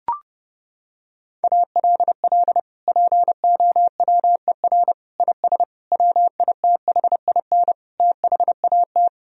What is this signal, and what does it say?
Morse code, heard as a keyed tone, about 30 wpm, 700 Hz. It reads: ALL POWER IS WITHIN THAT
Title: Sentences from Top 1000 Words - Review Part 4 - 30wpm